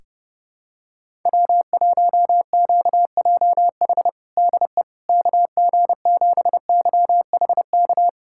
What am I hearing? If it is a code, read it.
Morse code, heard as a keyed tone, about 30 wpm, 700 Hz. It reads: W1QJH DE KG7YHK
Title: USA Callsign DE Callsign 30wpm